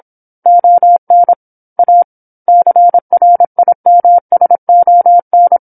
Morse code, heard as a keyed tone, about 27 wpm, 700 Hz. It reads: ON A CRIMSON
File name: 08-Chapter-08 26 WPM